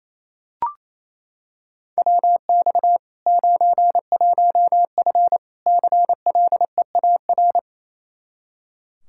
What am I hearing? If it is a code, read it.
Morse code, heard as a keyed tone, about 28 wpm, 700 Hz. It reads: WX 91F CLEAR
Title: K1USN SST QSO Practice 28wpm